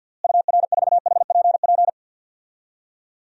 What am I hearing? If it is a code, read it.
Morse code, heard as a keyed tone, about 47 wpm, 700 Hz. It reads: UR4HPP
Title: DXCC-100 Call Signs - 6 Characters - Repeated 3x - 50wpm